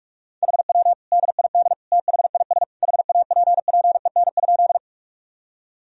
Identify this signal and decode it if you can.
Morse code, heard as a keyed tone, about 41 wpm, 700 Hz. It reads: HW DID THIS HAPPEN?
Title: Sentences from Top 400 Words and Abbreviations 45wpm